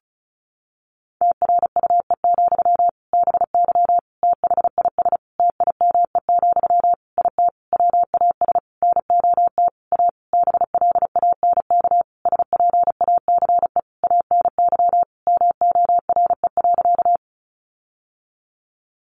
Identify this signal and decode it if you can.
Morse code, heard as a keyed tone, about 34 wpm, 700 Hz. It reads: TRUE, BY THIS TIME, IT WAS NOT A BLANK SPACE ANY MORE.
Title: Sentences from Top 1500 Words 35wpm